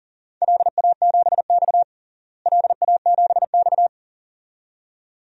Morse code, heard as a keyed tone, about 37 wpm, 700 Hz. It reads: LA7X LA7X
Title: DXCC-100 Call Signs - 4 Characters - Repeated 2x - 40wpm